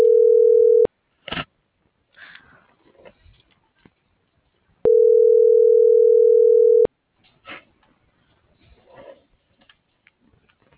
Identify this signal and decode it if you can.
Ringback tone (North America), 440 + 480 Hz.